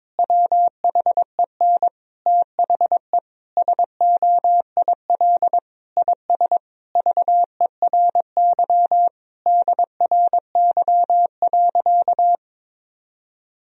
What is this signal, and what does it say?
Morse code, heard as a keyed tone, about 22 wpm, 700 Hz. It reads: WHEN THE SOIL IS VERY DRY.
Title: Sentences from Top 1000 Words - Review Part 4 - 22wpm